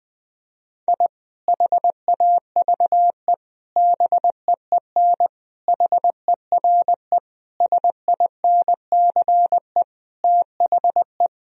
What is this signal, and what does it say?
Morse code, heard as a keyed tone, about 21 wpm, 700 Hz. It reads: I HAVE BEEN HERE SINCE THE
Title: Sentences from Top 1000 Words - Review Part 1 - 20wpm